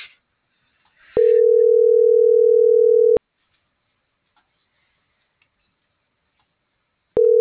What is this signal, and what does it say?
Ringback tone (North America), 440 + 480 Hz.
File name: CENTRAL NAVIERAS / out leonardo.pe.a